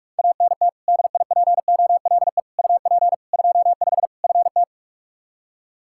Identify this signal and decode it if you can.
Morse code, heard as a keyed tone, about 42 wpm, 700 Hz. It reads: ANT DIPOLE UP 25 FT
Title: Practice QSO - Exchange 2 - Info - 45wpm